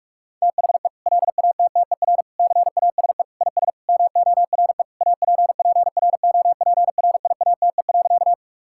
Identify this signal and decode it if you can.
Morse code, heard as a keyed tone, about 46 wpm, 700 Hz. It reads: THE LATTER CASE IS MORE APPROPRIATE.
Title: Sentences from Top 1600 Words 45wpm